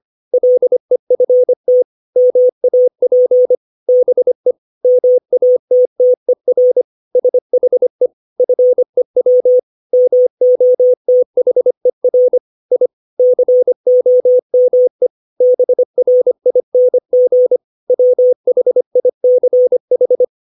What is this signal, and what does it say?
Morse code, heard as a keyed tone, about 25 wpm, 500 Hz. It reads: LEFT MAP BE MATTER SHE FEW MOTHER I COME BRING WHICH
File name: 632 words / 632 Words 25WPM